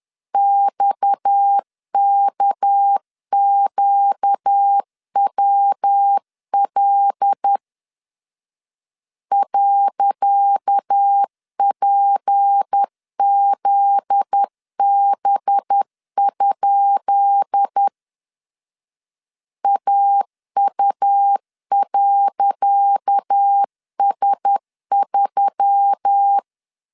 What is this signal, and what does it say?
Morse code, heard as a keyed tone, about 10 wpm, 800 Hz. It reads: XKQWL .PZB? AU.S3
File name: LEZIONI / 10 SIGNS / 10C DOT 10wpm 2940